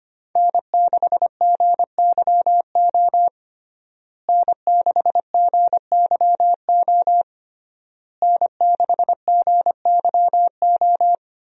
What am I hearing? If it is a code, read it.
Morse code, heard as a keyed tone, about 25 wpm, 700 Hz. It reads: N6GYO N6GYO N6GYO
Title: USA Call Signs - 5 Characters - Repeated 3x - 25wpm